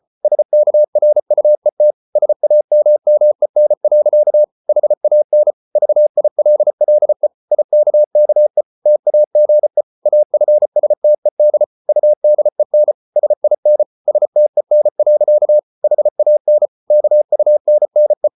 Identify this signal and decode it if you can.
Morse code, heard as a keyed tone, about 34 wpm, 600 Hz. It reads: SKRUET SAMMEN. HAN VILLE IKKE TAGE AFSTED UDEN SIN STEN. HAN KUNNE
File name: Ferie QRQ 34wpm